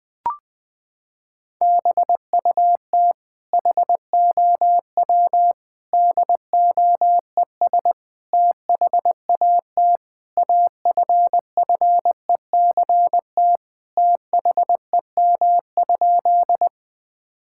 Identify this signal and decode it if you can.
Morse code, heard as a keyed tone, about 20 wpm, 700 Hz. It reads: BUT HOW DOES THAT AFFECT THEM?
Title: Sentences from Top 1000 Words - Review Part 6 - 20wpm